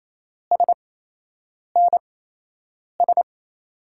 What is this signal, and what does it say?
Morse code, heard as a keyed tone, about 28 wpm, spaced as for 8 wpm, 700 Hz. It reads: SNS
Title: USA Call Signs - 5 Characters - 8x Character Spacing - 28wpm